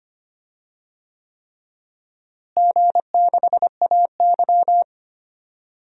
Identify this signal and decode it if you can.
Morse code, heard as a keyed tone, about 25 wpm, 700 Hz. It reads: G6AY
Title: DXCC-100 Call Signs - 4 Characters - Repeated 3x - 25wpm